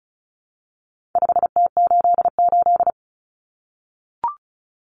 Morse code, heard as a keyed tone, about 34 wpm, 700 Hz. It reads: 5T88
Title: Common Cut Numbers - 4-Digit 35wpm